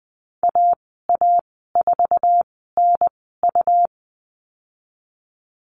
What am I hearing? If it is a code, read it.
Morse code, heard as a keyed tone, about 20 wpm, spaced as for 15 wpm, 700 Hz. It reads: AA4NU
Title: 128 Lesson 21g - Number 6 - Call Signs 20wpm